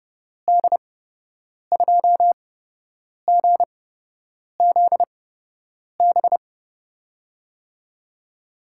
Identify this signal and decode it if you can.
Morse code, heard as a keyed tone, about 32 wpm, spaced as for 8 wpm, 700 Hz. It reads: D2GZB
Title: USA Call Signs - 8x Character Spacing - 30wpm